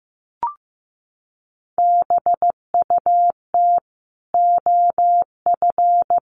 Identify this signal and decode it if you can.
Morse code, heard as a keyed tone, about 15 wpm, 700 Hz. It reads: BUT OF
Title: Sentences from Top 1500 Words 15wpm